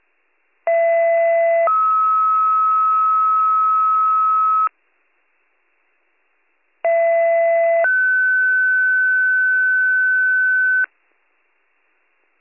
Two-tone page, heard as a one-second tone, then a three-second tone, about 670 then 1290 Hz, about 670 then 1500 Hz. It is heard from 0.7 to 10.9 s.